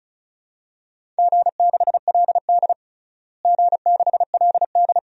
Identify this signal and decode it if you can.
Morse code, heard as a keyed tone, about 36 wpm, 700 Hz. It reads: G6LD G6LD
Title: DXCC-100 Call Signs - 4 Characters - Repeated 2x - 35wpm